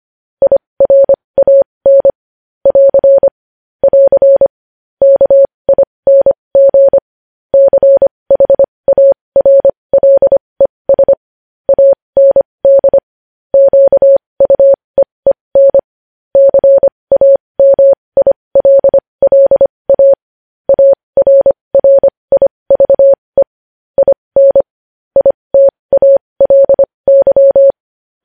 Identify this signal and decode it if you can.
Morse code, heard as a keyed tone, about 25 wpm, spaced as for 20 wpm, 550 Hz. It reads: IRAN <AR> <AR> KING CHARLES AND QUEEN CAMILLA ARRIVE IN ITALY
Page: news headlines in morse code at 20 wpm